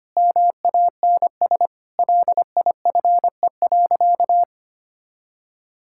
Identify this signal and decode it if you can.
Morse code, heard as a keyed tone, about 25 wpm, 700 Hz. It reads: MANS LIFE.
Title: Sentences from Top 1000 Words - Review Part 6 - 25wpm